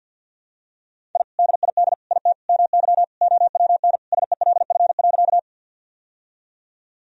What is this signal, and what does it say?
Morse code, heard as a keyed tone, about 52 wpm, 700 Hz. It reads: I DID IT MY OWN SELF.
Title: Sentences from Top 1000 Words - Review Part 1 - 50wpm